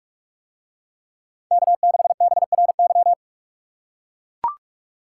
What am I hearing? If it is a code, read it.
Morse code, heard as a keyed tone, about 42 wpm, 700 Hz. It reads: K6BRY